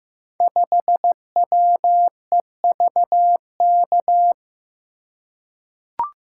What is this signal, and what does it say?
Morse code, heard as a keyed tone, about 15 wpm, 700 Hz. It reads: 5WEVK
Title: Code Groups - 5 Letter:Numbers - 15wpm